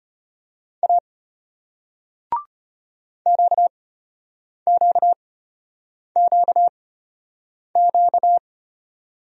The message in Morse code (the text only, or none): A QQQQ